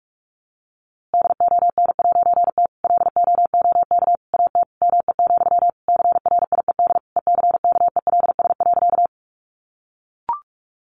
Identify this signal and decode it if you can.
Morse code, heard as a keyed tone, about 44 wpm, 700 Hz. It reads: DON'T LOOK AT ME, CRIED ECKELS.
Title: A Sound Of Thunder 45wpm